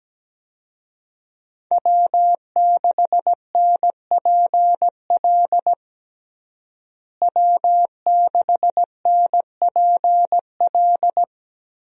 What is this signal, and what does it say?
Morse code, heard as a keyed tone, about 17 wpm, 700 Hz. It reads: W6NPL W6NPL
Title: USA Call Signs - 5 Characters - Repeated 3x - 17wpm